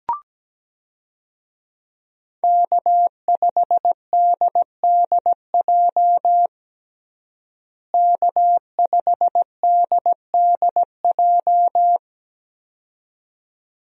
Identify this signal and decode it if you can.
Morse code, heard as a keyed tone, about 17 wpm, 700 Hz. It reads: K5DDJ K5DDJ
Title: USA Call Signs - 5 Characters - Repeated 2x - 17wpm